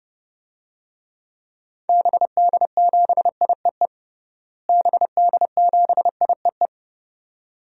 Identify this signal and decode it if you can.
Morse code, heard as a keyed tone, about 30 wpm, 700 Hz. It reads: BD7IEE BD7IEE
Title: DXCC-100 Call Signs - 6 Characters - Repeated 2x - 30wpm